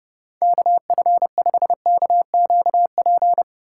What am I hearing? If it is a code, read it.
Morse code, heard as a keyed tone, about 30 wpm, 700 Hz. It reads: KF5KQP
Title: USA Call Signs - 6 Characters - 3x Character Spacing - 30wpm